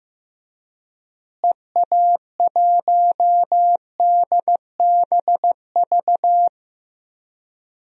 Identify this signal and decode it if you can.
Morse code, heard as a keyed tone, about 15 wpm, 700 Hz. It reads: EA1DBV
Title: DXCC-100 Call Signs - 6 Characters - Repeated 3x - 15wpm